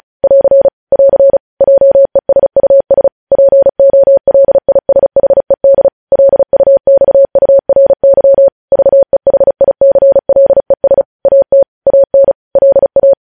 Morse code, heard as a keyed tone, about 36 wpm, 550 Hz. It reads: <AR> <AR> JESUS POLISHED LUXURY VEHICLES AT AN LA
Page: news headlines in morse code at 35 wpm